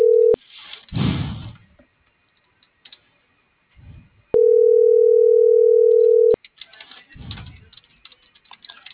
Ringback tone (North America), 440 + 480 Hz.